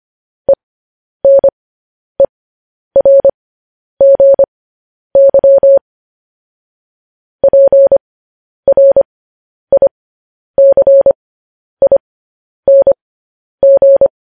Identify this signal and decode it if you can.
Morse code, heard as a keyed tone, about 25 wpm, spaced as for 10 wpm, 550 Hz. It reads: ENERGY PRICING
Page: news headlines in morse code at 10 wpm